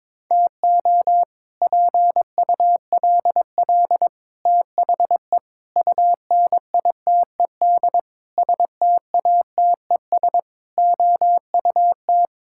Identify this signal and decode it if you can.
Morse code, heard as a keyed tone, about 22 wpm, 700 Hz. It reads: TO PULL THE UNITED STATES OUT
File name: practice / 22wpm / Sets of 8 Words - 22wpm